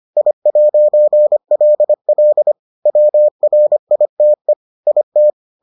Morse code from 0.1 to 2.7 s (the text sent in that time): I'LL